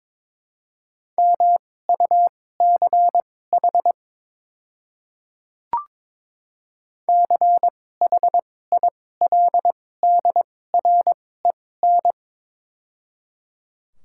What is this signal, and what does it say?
Morse code, heard as a keyed tone, about 22 wpm, spaced as for 16 wpm, 700 Hz. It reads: MUCH CHILDREN
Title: 2x Character Spacing - Top 300 Words - 22wpm